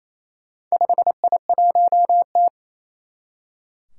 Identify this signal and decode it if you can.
Morse code, heard as a keyed tone, about 28 wpm, 700 Hz. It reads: 5I1T
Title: Four Challenging Letter-Number Combinations 28wpm